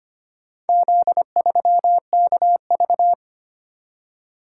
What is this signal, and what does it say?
Morse code, heard as a keyed tone, about 25 wpm, 700 Hz. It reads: Z3KV
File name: practice / 25wpm / Four Letter-Number Combinations - 25wpm